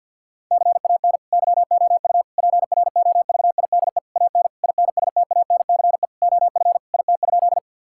Morse code, heard as a keyed tone, about 52 wpm, 700 Hz. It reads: CAN YOU PROVIDE AN INSTANCE OF IT?
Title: Sentences from Top 1000 Words - Review Part 5 - 50wpm